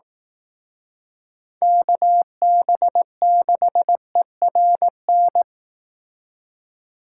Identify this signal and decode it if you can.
Morse code, heard as a keyed tone, about 18 wpm, 700 Hz. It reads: KB6ERN
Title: Speed Racing - USA 2x3 Call Signs 15wpm